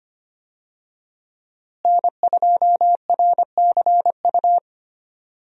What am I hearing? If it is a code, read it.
Morse code, heard as a keyed tone, about 25 wpm, 700 Hz. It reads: N2RCU
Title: USA Call Signs - 5 Characters - Repeated 2x - 25wpm